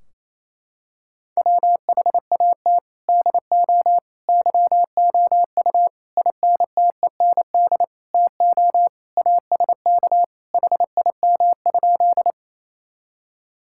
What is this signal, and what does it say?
Morse code, heard as a keyed tone, about 28 wpm, 700 Hz. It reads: WHAT DO YOU INTEND TO ASK HIM?
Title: Sentences from Top 1600 Words 28wpm